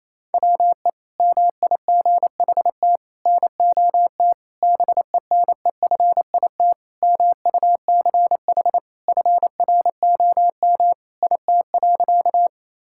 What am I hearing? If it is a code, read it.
Morse code, heard as a keyed tone, about 28 wpm, 700 Hz. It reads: WE MIGHT NOT BENEFIT MUCH FROM IT.
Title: Sentences from Top 1000 Words - Review Part 6 - 28wpm